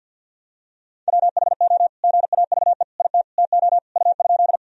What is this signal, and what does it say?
Morse code, heard as a keyed tone, about 50 wpm, 700 Hz. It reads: WHO GAVE IT TO U?
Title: Sentences from Top 500 Words and Abbreviations 50wpm